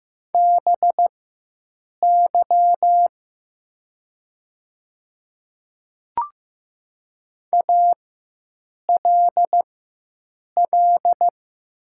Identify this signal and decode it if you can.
Morse code, heard as a keyed tone, about 15 wpm, spaced as for 7 wpm, 700 Hz. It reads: BY ALL
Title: 4x Character Spacing - Top 100 Words - 15wpm